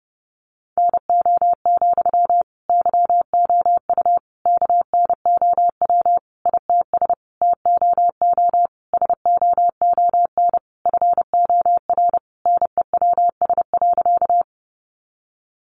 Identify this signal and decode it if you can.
Morse code, heard as a keyed tone, about 30 wpm, 700 Hz. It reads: NO, YOU KNOW ITS TOO SOON FOR NEWS.
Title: Sentences from Top 800 Words 30wpm